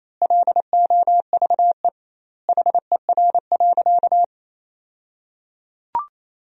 Morse code, heard as a keyed tone, about 28 wpm, 700 Hz. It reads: LOVE HER.
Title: 2x Word-Spacing - Sentences from Top 400 Words - 28wpm